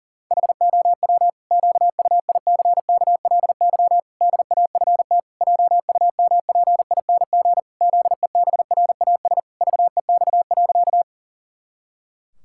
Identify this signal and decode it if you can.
Morse code, heard as a keyed tone, about 40 wpm, 700 Hz. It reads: HOW QUICKLY DAFT JUMPING ZEBRAS VEX.
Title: Pangram Challenge v2 - 40wpm